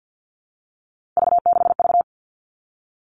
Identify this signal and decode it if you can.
Morse code, heard as a keyed tone, about 50 wpm, 700 Hz. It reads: V6V
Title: All Cut Numbers - 3-Digit 50wpm